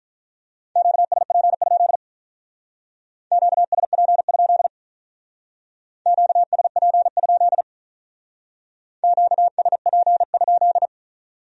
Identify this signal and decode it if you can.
Morse code, heard as a keyed tone, about 44 wpm, 700 Hz. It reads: QSP? QSP? QSP? QSP?
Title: Speed-Racing Q-Codes - 35wpm